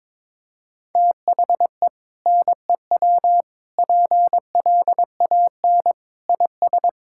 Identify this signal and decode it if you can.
Morse code, heard as a keyed tone, about 22 wpm, 700 Hz. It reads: THE NEW PLAN IS
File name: practice / 22wpm / Sentences with 2-Word Phrases - 22wpm